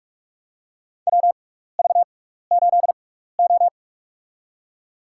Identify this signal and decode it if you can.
Morse code, heard as a keyed tone, about 43 wpm, spaced as for 15 wpm, 700 Hz. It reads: WV8O
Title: USA Call Signs - 4 Characters - 6x Character Spacing - 45wpm